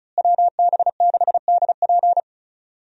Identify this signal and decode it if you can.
Morse code, heard as a keyed tone, about 34 wpm, 700 Hz. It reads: WB6DP
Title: USA Call Signs - 5 Characters - Repeated 3x - 35wpm